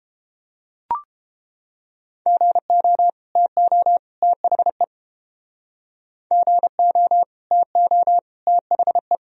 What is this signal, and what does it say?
Morse code, heard as a keyed tone, about 30 wpm, 700 Hz. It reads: GO TO THE GO TO THE